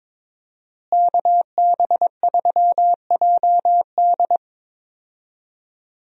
Morse code, heard as a keyed tone, about 22 wpm, 700 Hz. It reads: KB3JD
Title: USA Call Signs - 5 Characters - Repeated 3x - 22wpm